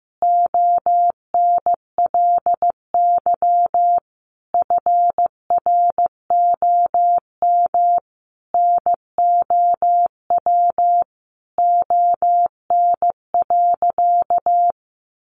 Morse code, heard as a keyed tone, about 15 wpm, 700 Hz. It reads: ONLY FROM NOW ON.
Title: Easy Phrases 15wpm